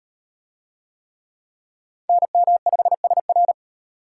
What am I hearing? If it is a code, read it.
Morse code, heard as a keyed tone, about 38 wpm, 700 Hz. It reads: NM5SR